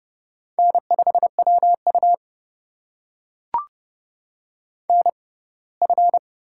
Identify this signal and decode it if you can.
Morse code, heard as a keyed tone, about 29 wpm, spaced as for 20 wpm, 700 Hz. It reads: N5WU NF
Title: USA Call Signs - 4 Characters - 6x Character Spacing - 30wpm